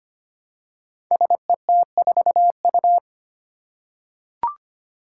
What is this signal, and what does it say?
Morse code, heard as a keyed tone, about 25 wpm, 700 Hz. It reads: SET4U
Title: Five Challenging Letter-Number Combinations 25wpm